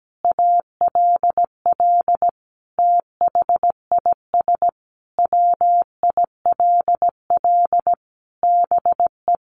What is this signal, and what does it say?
Morse code, heard as a keyed tone, about 17 wpm, 700 Hz. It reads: ALL THIS WILL BE
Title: Sentences from Top 1300 Words 17wpm